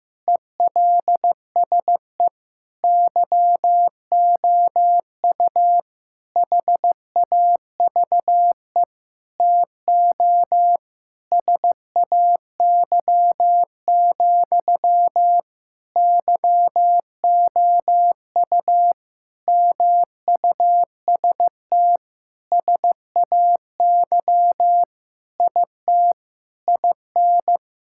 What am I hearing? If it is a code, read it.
Morse code, heard as a keyed tone, about 15 wpm, 700 Hz. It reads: ELSE YOU HAVE TO SAY, YOU MUST SAY IT IN
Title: Sentences from Top 1600 Words 15wpm